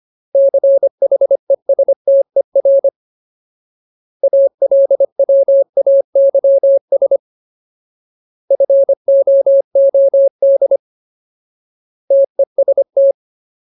Morse code, heard as a keyed tone, about 25 wpm, 550 Hz. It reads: CHESTER ALWAYS FOOD TEST
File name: Ultimate Word List / 25WPM Ultimate